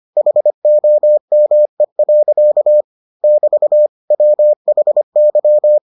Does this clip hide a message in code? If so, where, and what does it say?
Yes: Morse code from 0.2 to 5.8 s, heard as a keyed tone, about 25 wpm, 600 Hz. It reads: HOME. <BT> WHY